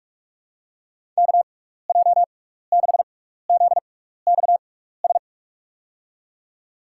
Morse code, heard as a keyed tone, about 46 wpm, spaced as for 16 wpm, 700 Hz. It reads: KJ6ZXS
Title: USA Call Signs - 6x Character Spacing - 45wpm